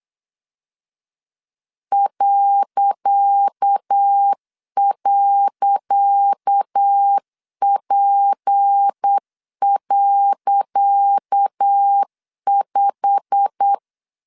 Morse code, heard as a keyed tone, about 8 wpm, 800 Hz. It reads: ..P.5
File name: LEZIONI / 10 SIGNS / 10C DOT 8wpm 3649